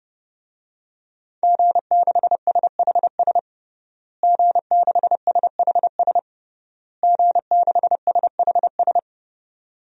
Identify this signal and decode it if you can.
Morse code, heard as a keyed tone, about 32 wpm, 700 Hz. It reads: G6SHS G6SHS G6SHS
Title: DXCC-100 Call Signs - 5 Characters - Repeated 3x - 30wpm